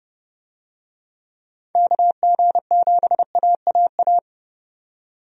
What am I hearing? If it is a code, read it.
Morse code, heard as a keyed tone, about 30 wpm, 700 Hz. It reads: KG7AAA
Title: USA Call Signs - 6 Characters - 30wpm